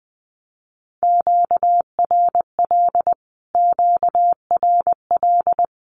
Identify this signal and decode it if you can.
Morse code, heard as a keyed tone, about 20 wpm, 700 Hz. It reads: QRL QRL
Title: Q-Codes Repeated 3x 20wpm